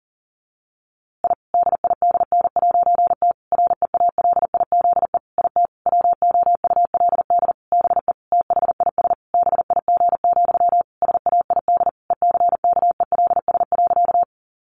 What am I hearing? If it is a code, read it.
Morse code, heard as a keyed tone, about 37 wpm, 700 Hz. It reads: I DIDN'T REALIZE IT WOULD BE THIS BIG, SAID ECKELS.
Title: A Sound Of Thunder 40wpm